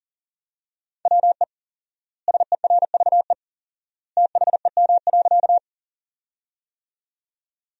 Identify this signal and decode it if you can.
Morse code, heard as a keyed tone, about 37 wpm, 700 Hz. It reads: WE SERVE THEM.
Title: 4x Word-Spacing - Sentences from Top 400 Words - 40wpm